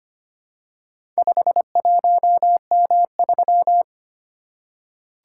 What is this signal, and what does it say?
Morse code, heard as a keyed tone, about 25 wpm, 690 Hz. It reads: 51M3
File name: practice / 25wpm / Four Letter-Number Combinations - 25wpm